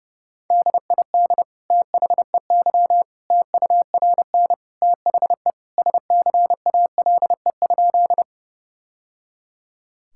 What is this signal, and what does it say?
Morse code, heard as a keyed tone, about 30 wpm, 700 Hz. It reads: DID THEY TURN THE SCALE?
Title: Sentences from Top 1000 Words - Review Part 5 - 30wpm